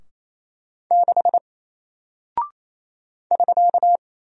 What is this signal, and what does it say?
Morse code, heard as a keyed tone, about 27 wpm, 700 Hz. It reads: 6 <SK>
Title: Single Character (Letter:Number:Punctuation:Prosign) Alt 28wpm